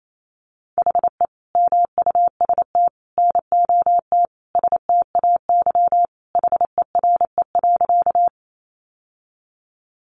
Morse code, heard as a keyed tone, about 28 wpm, 700 Hz. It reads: HE MUST NOT STAY HERE.
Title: Sentences from Top 600 Words 28wpm